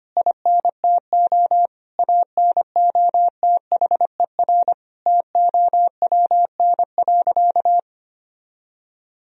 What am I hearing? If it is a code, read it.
Morse code, heard as a keyed tone, about 25 wpm, 700 Hz. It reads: INTO ANOTHER TOWN.
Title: Sentences from Top 1000 Words - Review Part 2 - 25wpm